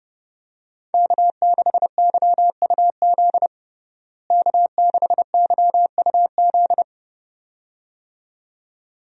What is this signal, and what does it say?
Morse code, heard as a keyed tone, about 30 wpm, 700 Hz. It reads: K6YUZ K6YUZ
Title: USA Call Signs - 5 Characters - Repeated 2x - 30wpm